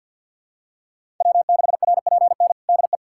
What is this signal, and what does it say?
Morse code, heard as a keyed tone, about 53 wpm, 700 Hz. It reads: W6RPN DE
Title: USA Callsign DE Callsign - 5 Characters - 50wpm